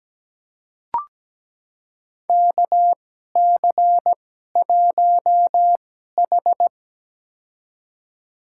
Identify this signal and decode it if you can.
Morse code, heard as a keyed tone, about 17 wpm, spaced as for 12 wpm, 700 Hz. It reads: KC1H
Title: USA Call Signs - 4 Characters - 2x Character Spacing - 17wpm